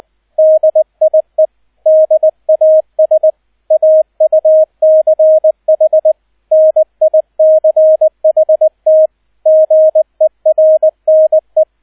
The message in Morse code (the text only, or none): DIE DAS AUCH NICHT GERNE